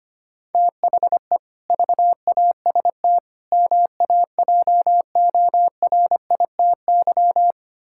Morse code, heard as a keyed tone, about 25 wpm, 700 Hz. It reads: THE VAST MAJORITY
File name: practice / 25wpm / Sentences with 4-Word Phrases - 25wpm